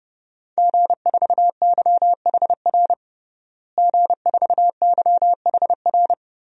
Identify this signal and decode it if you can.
Morse code, heard as a keyed tone, about 32 wpm, 700 Hz. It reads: G4YHR G4YHR
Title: DXCC-100 Call Signs - 5 Characters - Repeated 3x - 30wpm